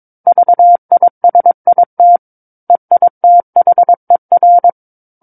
Morse code, heard as a keyed tone, about 22 wpm, 700 Hz. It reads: VISIT EITHER
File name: 06-Chapter-06 22 WPM